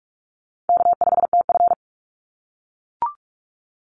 Morse code, heard as a keyed tone, about 44 wpm, 700 Hz. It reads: K5TF
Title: Easy Call Signs - 45wpm